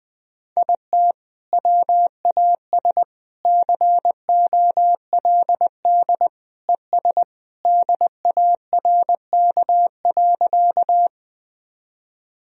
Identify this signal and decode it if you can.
Morse code, heard as a keyed tone, about 21 wpm, 700 Hz. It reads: IT WAS COLD ES DARK.
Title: Sentences from Top 500 Words and Abbreviations 20wpm